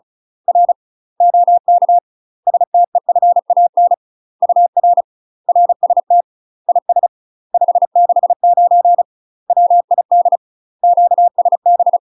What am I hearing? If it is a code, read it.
Morse code, heard as a keyed tone, about 36 wpm, 700 Hz. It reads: R OK STEFAN UR RST IS 569 WID QSB
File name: UK MASTER QSOs TRAINING / 35WPM / UK MASTER QSOs 0031